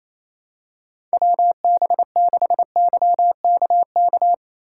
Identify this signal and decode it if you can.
Morse code, heard as a keyed tone, about 28 wpm, 700 Hz. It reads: WB6YKK
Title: USA Call Signs - 6 Characters - 2x Character Spacing - 28wpm